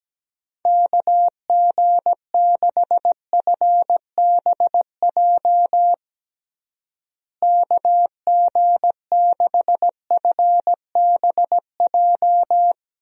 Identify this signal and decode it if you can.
Morse code, heard as a keyed tone, about 17 wpm, 700 Hz. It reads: KG6FBJ KG6FBJ
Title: USA Call Signs - 6 Characters - Repeated 2x - 17wpm